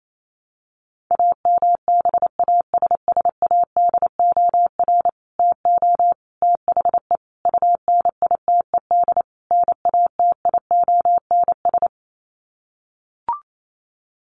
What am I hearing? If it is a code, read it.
Morse code, heard as a keyed tone, about 28 wpm, 700 Hz. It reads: AMBASSADOR TO THE UNITED NATIONS
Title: Sets of 5 Words Encore2 28wpm